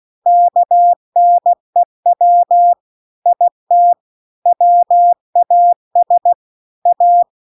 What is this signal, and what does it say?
Morse code, heard as a keyed tone, about 16 wpm, 700 Hz. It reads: KNEW IT WAS A
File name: CW-The-Big-Twoheaded-River-2-16WPM